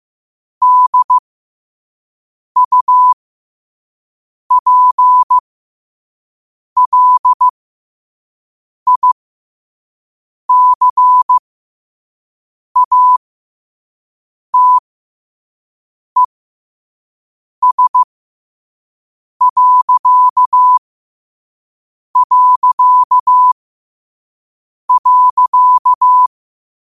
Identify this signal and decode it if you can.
Morse code, heard as a keyed tone, about 15 wpm, spaced as for 5 wpm, 1000 Hz. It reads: DUPLICATES...
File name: qotdfiles / 5wpm